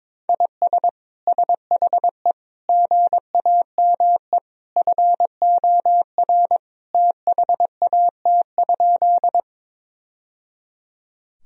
Morse code, heard as a keyed tone, about 22 wpm, 700 Hz. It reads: IS SHE GAME FOR THAT?
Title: Sentences from Top 1000 Words - Review Part 3 - 22wpm